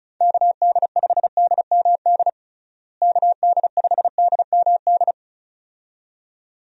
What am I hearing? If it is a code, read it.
Morse code, heard as a keyed tone, about 34 wpm, 700 Hz. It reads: KD5DMD KD5DMD
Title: USA Call Signs - 6 Characters - Repeated 2x - 35wpm